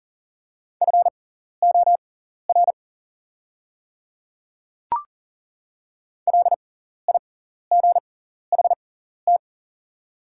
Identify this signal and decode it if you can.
Morse code, heard as a keyed tone, about 40 wpm, spaced as for 14 wpm, 690 Hz. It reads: FOR LIGHT
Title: 6x Character Spacing - Top 300 Words - 40wpm